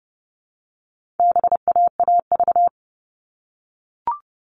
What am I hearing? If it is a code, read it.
Morse code, heard as a keyed tone, about 30 wpm, 700 Hz. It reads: BAAV
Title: All Cut Numbers - 4-Digit 30wpm